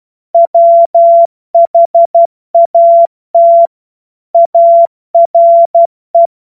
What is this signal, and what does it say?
Morse code, heard as a keyed tone, about 12 wpm, 680 Hz. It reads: WHAT ARE
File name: Morsekurs / Witze